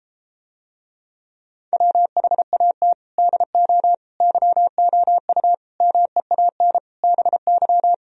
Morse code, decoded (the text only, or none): WHAT DO YOU MEAN BY